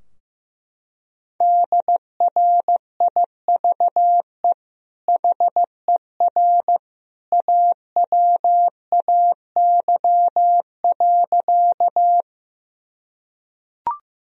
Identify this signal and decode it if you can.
Morse code, heard as a keyed tone, about 15 wpm, 700 Hz. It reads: DRIVE HER AWAY.
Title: Sentences from Top 1000 Words - Review Part 2 - 15wpm